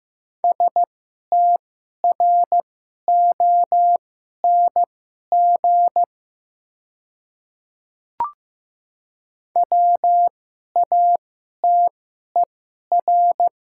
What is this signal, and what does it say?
Morse code, heard as a keyed tone, about 15 wpm, spaced as for 11 wpm, 690 Hz. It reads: STRONG WATER
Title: 2x Character Spacing - Top 500 Words - 15wpm